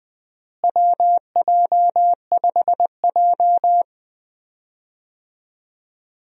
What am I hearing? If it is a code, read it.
Morse code, heard as a keyed tone, about 21 wpm, 700 Hz. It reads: WJ5J
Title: USA Call Signs - 4 Characters - Repeated 2x - 20wpm